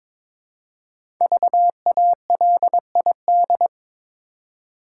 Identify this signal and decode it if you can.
Morse code, heard as a keyed tone, about 22 wpm, 700 Hz. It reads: VALID
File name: 5 Letter Words - First 3 Letters Given - 22wpm